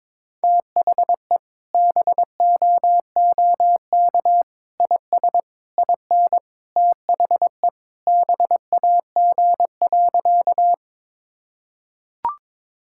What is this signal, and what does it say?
Morse code, heard as a keyed tone, about 22 wpm, 700 Hz. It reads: THE BOOK IS IN THE BAG.
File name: practice / 22wpm / Sentences with 3-Word Phrases - 22wpm